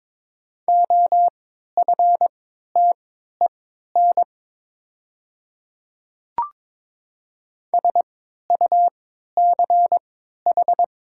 Morse code, heard as a keyed tone, about 22 wpm, spaced as for 12 wpm, 700 Hz. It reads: OFTEN SUCH
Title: 3x Character Spacing - Top 500 Words - 22wpm